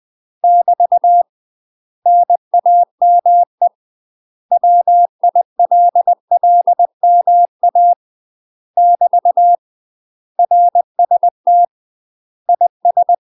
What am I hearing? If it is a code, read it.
Morse code, heard as a keyed tone, about 20 wpm, 700 Hz. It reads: <BT> NAME WILLMA <BT> RST IS
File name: UK MASTER QSOs TRAINING / 20WPM / UK MASTER QSOs 0060